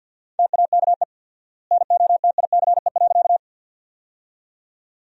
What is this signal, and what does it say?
Morse code, heard as a keyed tone, about 50 wpm, 700 Hz. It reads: TAKE NOTICE.
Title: 4x Word-Spacing - Sentences from Top 400 Words - 50wpm